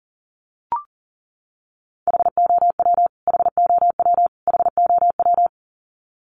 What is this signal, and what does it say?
Morse code, heard as a keyed tone, about 40 wpm, 700 Hz. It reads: HOW HOW HOW
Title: Top 300 Words Repeated 40wpm